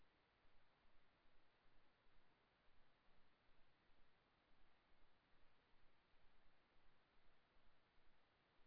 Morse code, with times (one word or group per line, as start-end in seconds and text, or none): none